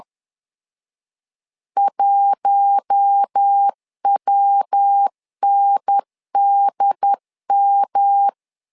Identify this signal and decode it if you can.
Morse code, heard as a keyed tone, about 10 wpm, 800 Hz. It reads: 1WNDM
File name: LEZIONI / 4 Q1 10wpm 2448